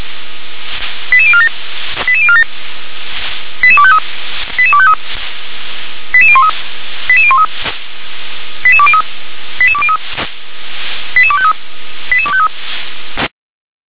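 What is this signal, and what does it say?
Five-tone selective call (ZVEI), heard as quick runs of five tones, tones of about 70 ms. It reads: 88047 88047 88253 88253 88013 88013 88203 88203 88253 88253
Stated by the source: Five-tone ZVEI selective call 88013 received via SDR.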